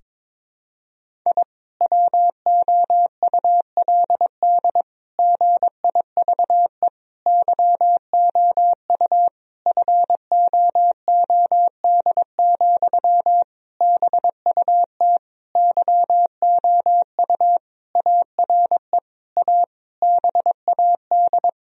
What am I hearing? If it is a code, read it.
Morse code, heard as a keyed tone, about 22 wpm, 700 Hz. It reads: I WOULD GIVE YOU FOOD, BUT YOU ARE A BAD
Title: Sentences from Top 1000 Words - Review Part 1 - 22wpm